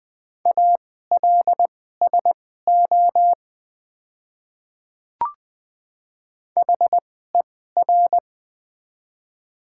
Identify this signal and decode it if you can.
Morse code, heard as a keyed tone, about 21 wpm, spaced as for 15 wpm, 700 Hz. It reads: ALSO HER
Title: 2x Character Spacing - Top 200 Words - 20wpm